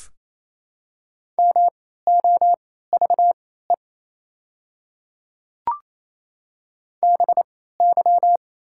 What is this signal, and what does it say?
Morse code, heard as a keyed tone, about 28 wpm, spaced as for 16 wpm, 690 Hz. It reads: MOVE BY
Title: 3x Character Spacing - Top 300 Words - 28wpm